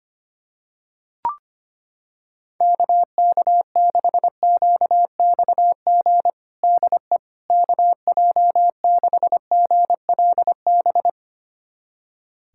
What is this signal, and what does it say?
Morse code, heard as a keyed tone, about 25 wpm, 700 Hz. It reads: KK6QXG DE KJ6GLB
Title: USA Callsign DE Callsign - 6 Characters - 25wpm